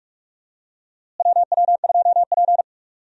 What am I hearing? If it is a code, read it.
Morse code, heard as a keyed tone, about 43 wpm, 700 Hz. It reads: WW2P
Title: USA Call Signs - 4 Characters - Repeated 2x - 45wpm